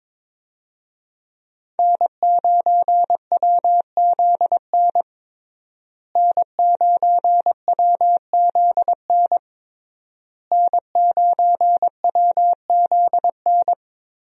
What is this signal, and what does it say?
Morse code, heard as a keyed tone, about 22 wpm, 700 Hz. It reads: N9WZN N9WZN N9WZN
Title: USA Call Signs - 5 Characters - Repeated 3x - 22wpm